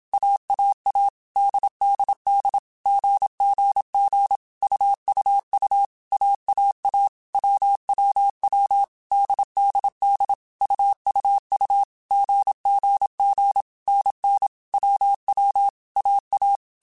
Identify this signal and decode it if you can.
Morse code, heard as a keyed tone, about 27 wpm, 800 Hz. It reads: AAA DDD GGG UUU AAA WWW DDD UUU GGG NN WW AA
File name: Basic Sets / Basic Letter Groups Collective7 / AWUNDG@25